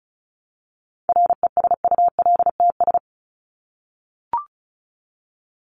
Morse code, heard as a keyed tone, about 34 wpm, 700 Hz. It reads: RESULTS
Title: ES:S Suffix - Top 100 Words with Base Words 35wpm